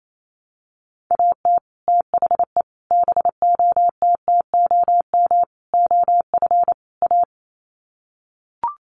Morse code, heard as a keyed tone, about 28 wpm, 700 Hz. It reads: AT THE BOTTOM OF A
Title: Sets of 5 Words Encore2 28wpm